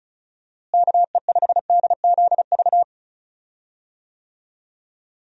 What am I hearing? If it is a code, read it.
Morse code, heard as a keyed tone, about 36 wpm, 700 Hz. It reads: KE5DZV
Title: USA Call Signs - 6 Characters - Repeated 2x - 35wpm